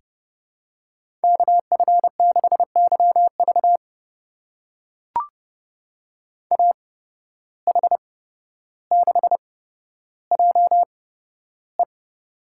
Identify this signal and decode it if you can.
Morse code, heard as a keyed tone, about 29 wpm, spaced as for 13 wpm, 700 Hz. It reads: KF6YV AH6JE
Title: USA Call Signs - 5 Characters - 8x Character Spacing - 30wpm